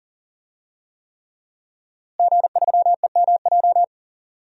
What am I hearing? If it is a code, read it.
Morse code, heard as a keyed tone, about 40 wpm, 700 Hz. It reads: G3EMJ
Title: DXCC-100 Call Signs - 5 Characters - Repeated 3x - 40wpm